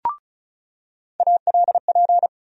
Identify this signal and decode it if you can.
Morse code, heard as a keyed tone, about 34 wpm, 700 Hz. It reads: ALP
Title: USA Call Sign Postfixes - 3 Letters 35wpm